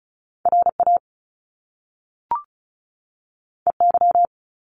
Morse code, heard as a keyed tone, about 36 wpm, 700 Hz. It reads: RA EY
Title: Top Two Letter Combinations in Words 35wpm